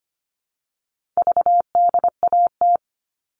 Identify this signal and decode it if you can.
Morse code, heard as a keyed tone, about 25 wpm, 700 Hz. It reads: VDAT
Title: All Cut Numbers - 4-Digit 25wpm